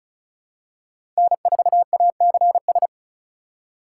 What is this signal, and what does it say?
Morse code, heard as a keyed tone, about 36 wpm, 700 Hz. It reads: N4ACS
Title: USA Call Signs - 5 Characters - 4x Character Spacing - 35wpm